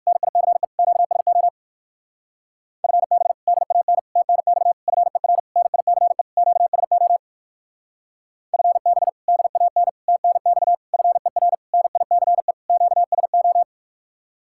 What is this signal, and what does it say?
Morse code, heard as a keyed tone, about 54 wpm, 700 Hz. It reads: NICE QSO FB DAN TNX FER NICE QSO FB DAN TNX FER NICE QSO
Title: Speed-Racing Common QSO Phrases - 45wpm